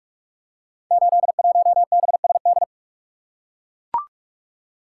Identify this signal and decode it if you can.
Morse code, heard as a keyed tone, about 44 wpm, 700 Hz. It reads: 81BSD
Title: Code Groups - 5 Letter:Numbers - 45wpm